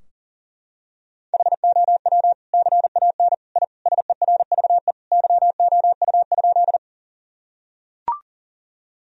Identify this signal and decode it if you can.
Morse code, heard as a keyed tone, about 40 wpm, 700 Hz. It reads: HOW CAN I SERVE YOU?
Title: Sentences from Top 1000 Words - Review Part 2 - 40wpm